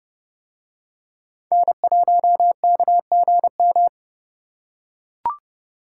Morse code, heard as a keyed tone, about 29 wpm, 700 Hz. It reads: N1KGM